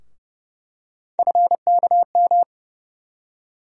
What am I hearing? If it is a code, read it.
Morse code, heard as a keyed tone, about 30 wpm, 700 Hz. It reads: FKM
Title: USA Call Sign Postfixes - 3 Letters 30wpm